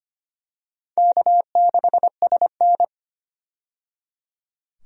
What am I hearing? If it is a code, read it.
Morse code, heard as a keyed tone, about 25 wpm, 700 Hz. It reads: K6SN
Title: USA Call Signs - 4 Characters - 25wpm